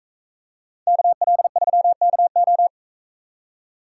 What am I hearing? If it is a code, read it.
Morse code, heard as a keyed tone, about 42 wpm, 700 Hz. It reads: KL3KO